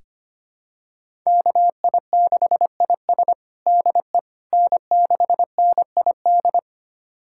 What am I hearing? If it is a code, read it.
Morse code, heard as a keyed tone, about 25 wpm, 700 Hz. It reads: KI6IS DE N6NID
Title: USA Callsign DE Callsign - 5 Characters - 25wpm